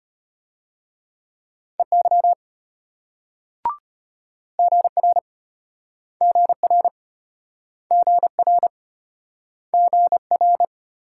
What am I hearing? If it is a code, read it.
Morse code, heard as a keyed tone, about 34 wpm, 700 Hz. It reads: EY GR GR GR GR